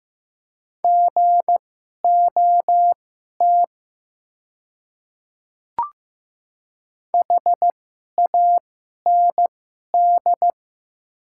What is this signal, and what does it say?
Morse code, heard as a keyed tone, about 15 wpm, spaced as for 11 wpm, 700 Hz. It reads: GOT HAND
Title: 2x Character Spacing - Top 300 Words - 15wpm